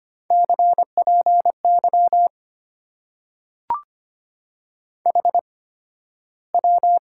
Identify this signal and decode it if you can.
Morse code, heard as a keyed tone, about 25 wpm, spaced as for 13 wpm, 700 Hz. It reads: CPY HW
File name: Top 90 QSO Elements - 8x Character Spacing - 25wpm